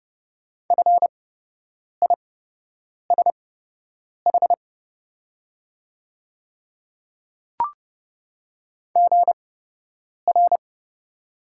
Morse code, heard as a keyed tone, about 30 wpm, spaced as for 8 wpm, 700 Hz. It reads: FISH GR
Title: 8x Character Spacing - Top 300 Words - 30wpm